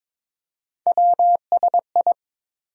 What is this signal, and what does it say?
Morse code, heard as a keyed tone, about 22 wpm, 700 Hz. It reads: WSI
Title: USA Call Sign Postfixes - 3 Letters 22wpm